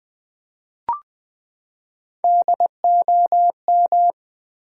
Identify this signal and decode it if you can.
Morse code, heard as a keyed tone, about 21 wpm, 700 Hz. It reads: DOM
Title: Build A Word - 3 Letters At A Time - 8-Letter Words - 20wpm